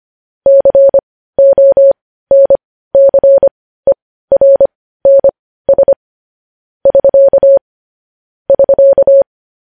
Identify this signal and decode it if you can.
Morse code, heard as a keyed tone, about 25 wpm, spaced as for 15 wpm, 550 Hz. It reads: CONCERNS <SK> <SK>